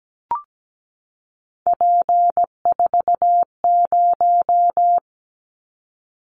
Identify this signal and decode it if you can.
Morse code, heard as a keyed tone, about 17 wpm, 700 Hz. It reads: P40
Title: Callsign Prefixes for Top 100 DXCC - 17wpm